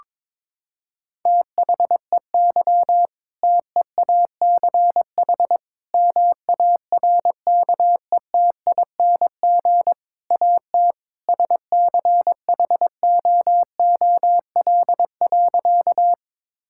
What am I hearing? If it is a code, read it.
Morse code, heard as a keyed tone, about 22 wpm, 700 Hz. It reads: THEY TEACH MARKETING AT SCHOOL.
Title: Sentences from Top 1000 Words - Review Part 5 - 22wpm